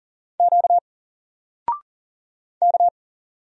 Morse code, heard as a keyed tone, about 40 wpm, 700 Hz. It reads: Q K
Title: Single Letter-Number Alt 40wpm